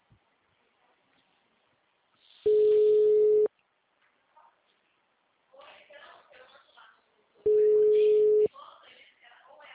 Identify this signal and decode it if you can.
Ringback tone (Europe, 425 Hz), with speech.